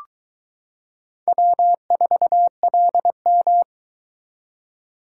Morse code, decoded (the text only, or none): W4LM